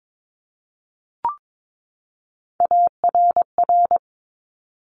Morse code, heard as a keyed tone, about 22 wpm, 700 Hz. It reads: ARR
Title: Top Three Letter Combinations in Words 22wpm